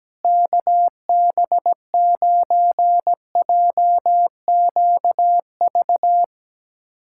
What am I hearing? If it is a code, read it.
Morse code, heard as a keyed tone, about 17 wpm, 700 Hz. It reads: KB9JQV
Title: USA Callsign DE Callsign 17wpm